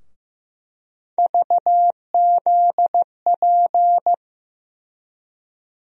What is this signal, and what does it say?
Morse code, heard as a keyed tone, about 15 wpm, 700 Hz. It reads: VZP
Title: USA Call Sign Postfixes - 3 Letters 15wpm